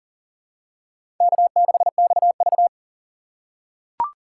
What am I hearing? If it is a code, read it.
Morse code, heard as a keyed tone, about 40 wpm, 700 Hz. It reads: K6XV
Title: USA Call Signs - 4 Characters - 3x Character Spacing - 40wpm